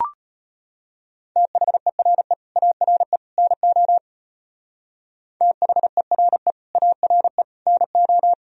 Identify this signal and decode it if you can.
Morse code, heard as a keyed tone, about 32 wpm, 700 Hz. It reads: THERE ARE NO THERE ARE NO